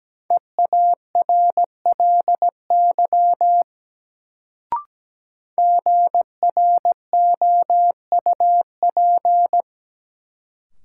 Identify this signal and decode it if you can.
Morse code, heard as a keyed tone, about 17 wpm, 700 Hz. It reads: EARLY GROUP
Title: Sets of 3 Words - New Word Drill - 17wpm